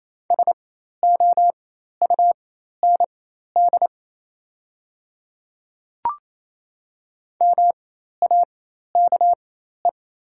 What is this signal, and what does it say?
Morse code, heard as a keyed tone, about 28 wpm, spaced as for 13 wpm, 690 Hz. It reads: SOUND MAKE
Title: 4x Character Spacing - Top 200 Words - 28wpm